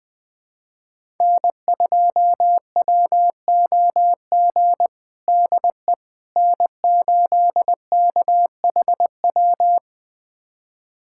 Morse code, heard as a keyed tone, about 21 wpm, 700 Hz. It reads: N2WOG DE N8KHW
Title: USA Callsign DE Callsign - 5 Characters - 20wpm